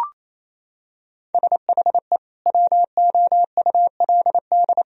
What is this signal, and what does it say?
Morse code, heard as a keyed tone, about 28 wpm, 700 Hz. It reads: SHE WOULD
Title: Sentences from Top 1000 Words - Review Part 2 - 28wpm